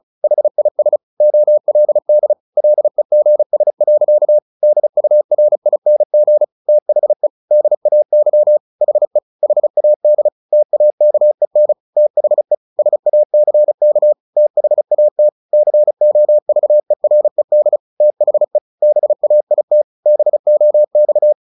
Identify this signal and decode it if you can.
Morse code, heard as a keyed tone, about 35 wpm, 600 Hz. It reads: HIS OLD LEGS. DURING THE DAY HE HAD TAKEN THE SACK THAT COVERED THE BAIT BOX
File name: the old man and the sea / the old man and the sea 1 35wpm